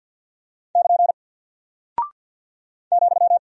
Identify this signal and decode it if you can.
Morse code, heard as a keyed tone, about 48 wpm, 700 Hz. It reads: <KN> ,
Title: Single Character (Letter:Number:Punctuation:Prosign) 50wpm